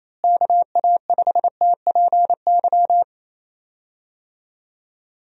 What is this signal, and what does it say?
Morse code, heard as a keyed tone, about 28 wpm, 700 Hz. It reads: KA5TPY
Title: USA Call Signs - 6 Characters - Repeated 3x - 28wpm